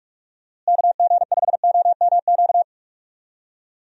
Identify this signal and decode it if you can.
Morse code, heard as a keyed tone, about 46 wpm, 700 Hz. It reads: KG5OMQ
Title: USA Call Signs - 6 Characters - Repeated 2x - 45wpm